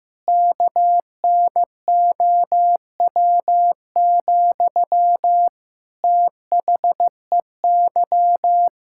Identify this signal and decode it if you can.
Morse code, heard as a keyed tone, about 15 wpm, 700 Hz. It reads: KNOW, THEY
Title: Sentences from Top 1000 Words - Review Part 5 - 15wpm